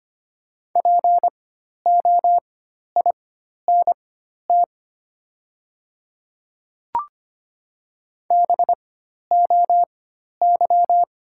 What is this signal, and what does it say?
Morse code, heard as a keyed tone, about 25 wpm, spaced as for 12 wpm, 700 Hz. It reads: POINT BOY
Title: 4x Character Spacing - Top 200 Words - 25wpm